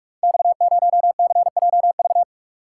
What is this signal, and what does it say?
Morse code, heard as a keyed tone, about 44 wpm, 700 Hz. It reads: X0CJV
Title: Code Groups - 5 Letter:Numbers - 45wpm